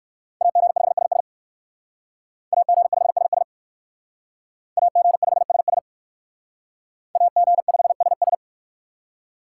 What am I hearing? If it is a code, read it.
Morse code, heard as a keyed tone, about 57 wpm, 700 Hz. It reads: AG5SS AG5SS AG5SS AG5SS